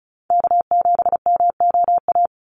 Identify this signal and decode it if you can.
Morse code, heard as a keyed tone, about 36 wpm, 700 Hz. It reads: K7MOA
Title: Call Signs - 35wpm